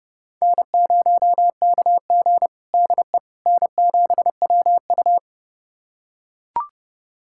Morse code, heard as a keyed tone, about 32 wpm, 700 Hz. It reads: N0KG DE N7WU
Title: USA Callsign DE Callsign - 4 Characters - 30wpm